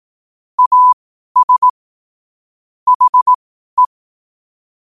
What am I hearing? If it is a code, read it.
Morse code, heard as a keyed tone, about 18 wpm, spaced as for 12 wpm, 1000 Hz. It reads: AS HE
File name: qotdfiles / 12wpm